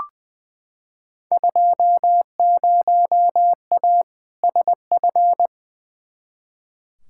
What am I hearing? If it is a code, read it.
Morse code, heard as a keyed tone, about 21 wpm, 700 Hz. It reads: 20A SF
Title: Field Day Exchanges Encore - 20wpm